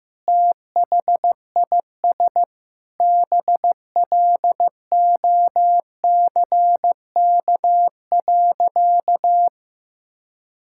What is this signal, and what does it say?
Morse code, heard as a keyed tone, about 15 wpm, 700 Hz. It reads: THIS BLOCK.
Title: Sentences from Top 1600 Words 15wpm